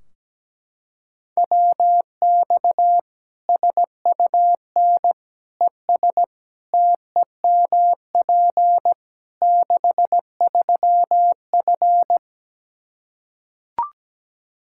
Morse code, heard as a keyed tone, about 17 wpm, 700 Hz. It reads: WX SUN ES TEMP 63F
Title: Practice QSO - Exchange 2 - Info - 17wpm